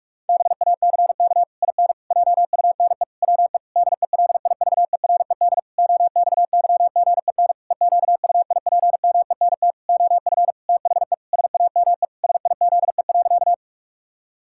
Morse code, heard as a keyed tone, about 46 wpm, 700 Hz. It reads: BACK IN JUNE WE DELIVERED OXYGEN EQUIPMENT OF THE SAME SIZE.
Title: Pangram Challenge v2 - 45wpm